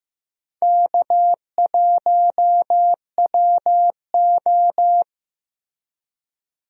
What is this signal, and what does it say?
Morse code, heard as a keyed tone, about 15 wpm, 700 Hz. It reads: K1WO
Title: USA Call Signs - 4 Characters - Repeated 2x - 15wpm